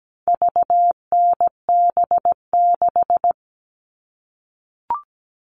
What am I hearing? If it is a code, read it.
Morse code, heard as a keyed tone, about 17 wpm, 700 Hz. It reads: VNB6
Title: All Cut Numbers - 4-Digit 17wpm